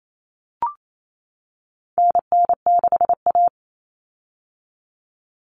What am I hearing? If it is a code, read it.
Morse code, heard as a keyed tone, about 28 wpm, 700 Hz. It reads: NN6A
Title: Common Cut Numbers - 4-Digit 28wpm